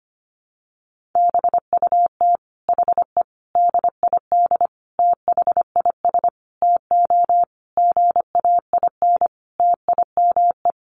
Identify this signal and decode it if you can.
Morse code, heard as a keyed tone, about 25 wpm, 700 Hz. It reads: BUT HE DID THIS TO GAIN TIME
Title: Sentences from Top 1400 Words 25wpm